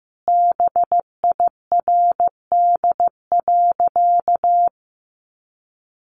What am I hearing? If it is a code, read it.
Morse code, heard as a keyed tone, about 15 wpm, 700 Hz. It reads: BIRD.
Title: Sentences from Top 1500 Words 15wpm